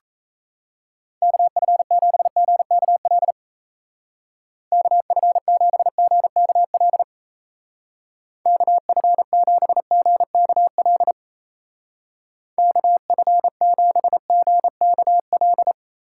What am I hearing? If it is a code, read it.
Morse code, heard as a keyed tone, about 37 wpm, 700 Hz. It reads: KF7GKL KF7GKL KF7GKL KF7GKL